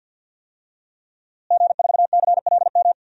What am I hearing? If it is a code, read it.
Morse code, heard as a keyed tone, about 53 wpm, 700 Hz. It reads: G4CLM
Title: DXCC-100 Call Signs - 5 Characters - Repeated 3x - 50wpm